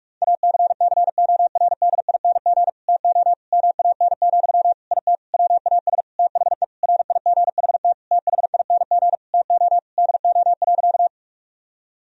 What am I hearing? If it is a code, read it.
Morse code, heard as a keyed tone, about 43 wpm, 700 Hz. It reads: ACCORDING TO MAN, IT WAS THE RIGHT THING TO DO.
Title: Sentences from Top 1000 Words - Review Part 4 - 45wpm